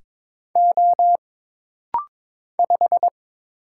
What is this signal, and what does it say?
Morse code, heard as a keyed tone, about 22 wpm, 700 Hz. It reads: O 5
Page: This rapid-fire timing is geared to push your ICR (Instant Character Recognition) skills.